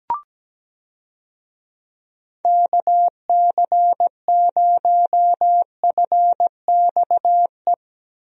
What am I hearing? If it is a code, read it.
Morse code, heard as a keyed tone, about 17 wpm, 700 Hz. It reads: KC0FXE
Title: USA Call Signs - 6 Characters - Repeated 2x - 17wpm